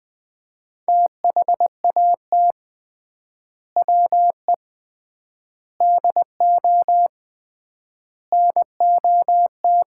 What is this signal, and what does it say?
Morse code, heard as a keyed tone, about 20 wpm, 700 Hz. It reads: THAT WE DO NOT
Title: 3x Word-Spacing - Sets of 4 Words - 20wpm